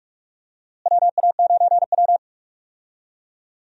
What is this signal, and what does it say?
Morse code, heard as a keyed tone, about 44 wpm, 700 Hz. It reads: WA9W
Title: USA Call Signs - 4 Characters - Repeated 3x - 45wpm